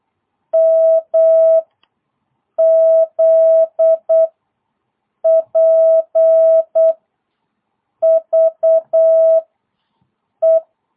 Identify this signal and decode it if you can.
Morse code, heard as a keyed tone, about 8 wpm, 660 Hz. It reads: M Z P V E